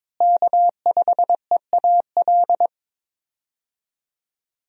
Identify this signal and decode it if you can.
Morse code, heard as a keyed tone, about 22 wpm, 700 Hz. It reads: K5EAL
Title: USA Call Signs - 5 Characters - Repeated 3x - 22wpm